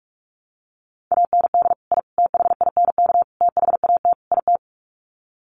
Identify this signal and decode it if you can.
Morse code, heard as a keyed tone, about 44 wpm, 700 Hz. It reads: AND I THINK THAT IT
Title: Sets of 5 Words Encore2 45wpm